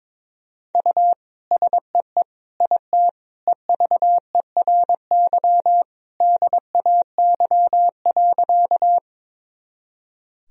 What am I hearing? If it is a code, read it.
Morse code, heard as a keyed tone, about 22 wpm, 700 Hz. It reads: U SEE IT EVERY DAY.
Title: Sentences from Top 200 Words and Abbreviations 22wpm